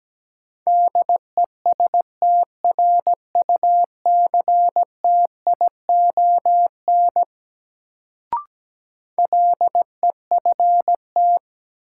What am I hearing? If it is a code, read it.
Morse code, heard as a keyed tone, about 17 wpm, 700 Hz. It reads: DESTRUCTION LEFT
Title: Sets of 5 Words Encore - New Word Drill - 17wpm